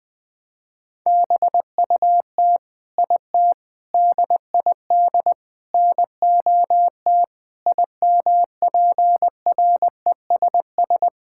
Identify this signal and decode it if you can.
Morse code, heard as a keyed tone, about 20 wpm, 700 Hz. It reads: BUT IT DID NOT IMPRESS
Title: Sentences from Top 1600 Words 20wpm